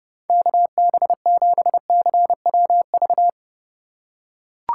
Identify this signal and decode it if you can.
Morse code, heard as a keyed tone, about 30 wpm, 700 Hz. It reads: KB7CWV
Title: USA Call Signs - 6 Characters - 2x Character Spacing - 30wpm